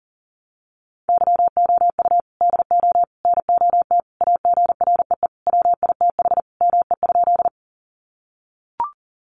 Morse code, heard as a keyed tone, about 40 wpm, 700 Hz. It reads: YOU DO NOT AGREE WITH ME?
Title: Sentences from Top 1100 Words 40wpm